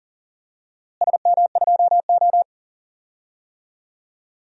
Morse code, heard as a keyed tone, about 39 wpm, 700 Hz. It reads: SM2O
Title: DXCC-100 Call Signs - 4 Characters - Repeated 3x - 40wpm